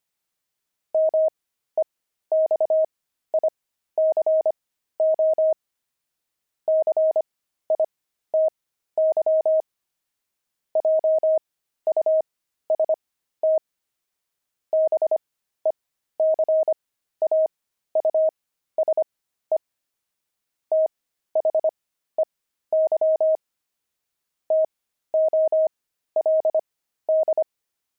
Morse code, heard as a keyed tone, about 25 wpm, spaced as for 13 wpm, 630 Hz. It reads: MEXICO CITY JUST BECAUSE THEY TOLD
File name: pangram-fs-13-cs-25-kidding